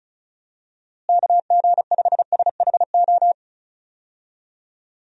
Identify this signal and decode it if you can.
Morse code, heard as a keyed tone, about 34 wpm, 700 Hz. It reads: KG5SHO
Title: USA Call Signs - 6 Characters - Repeated 2x - 35wpm